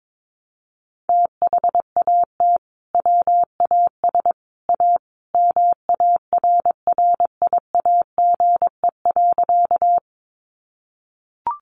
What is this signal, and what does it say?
Morse code, heard as a keyed tone, about 22 wpm, 700 Hz. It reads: THAT WAS A MARRIAGE.
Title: Sentences from Top 800 Words 22wpm